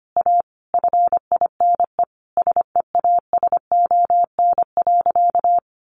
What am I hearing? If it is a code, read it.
Morse code, heard as a keyed tone, about 25 wpm, 700 Hz. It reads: A FINE SEASON.
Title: Sentences from Top 1400 Words 25wpm